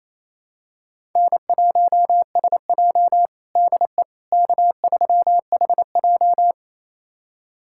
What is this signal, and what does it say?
Morse code, heard as a keyed tone, about 28 wpm, 700 Hz. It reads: N1SJ DE K3HJ
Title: USA Callsign DE Callsign - 4 Characters - 28wpm